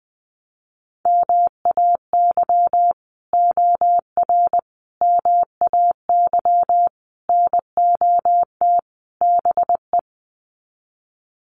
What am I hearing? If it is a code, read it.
Morse code, heard as a keyed tone, about 20 wpm, 700 Hz. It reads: MAY OR MAY NOT BE
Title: Sets of 5 Words 20wpm